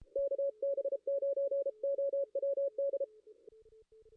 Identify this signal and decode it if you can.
Morse code, heard as a keyed tone, about 33 wpm, 560 Hz. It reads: KB9OWD
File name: contests / ss-cw-k5tr-op-n5rz -2300